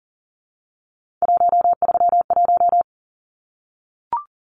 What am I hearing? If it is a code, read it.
Morse code, heard as a keyed tone, about 38 wpm, 700 Hz. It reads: 131
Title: Numbers - 3-Digit 40wpm